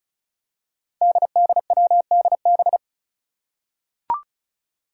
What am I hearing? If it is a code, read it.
Morse code, heard as a keyed tone, about 36 wpm, 700 Hz. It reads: DDWDB
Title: Five Challenging Letter-Number Combinations 35wpm